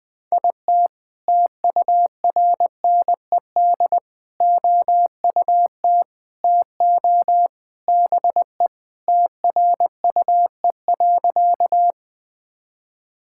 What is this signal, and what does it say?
Morse code, heard as a keyed tone, about 21 wpm, 700 Hz. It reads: IT TURNED OUT TO BE TRUE.
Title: Sentences with 3-Word Phrases - 20wpm